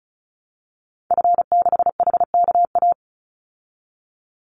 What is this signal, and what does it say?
Morse code, heard as a keyed tone, about 34 wpm, 700 Hz. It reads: F6HKA
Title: Call Signs Repeated 2x - 35wpm